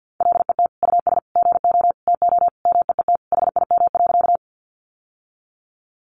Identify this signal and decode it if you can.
Morse code, heard as a keyed tone, about 50 wpm, 700 Hz. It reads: LET US GO TO MEET HIM.
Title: Sentences from Top 1100 Words 50wpm